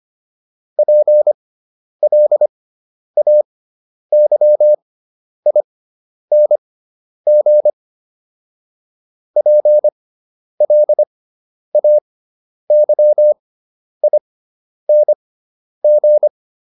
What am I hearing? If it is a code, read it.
Morse code, heard as a keyed tone, about 25 wpm, spaced as for 10 wpm, 600 Hz. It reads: PLAYING PLAYING